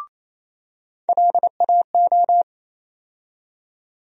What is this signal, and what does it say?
Morse code, heard as a keyed tone, about 28 wpm, 700 Hz. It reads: LAO
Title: USA Call Sign Postfixes - 3 Letters 28wpm